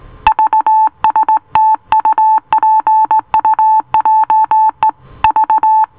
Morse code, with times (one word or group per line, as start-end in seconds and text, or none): 0.3-4.9 VSTUPUJE
5.2-5.8 V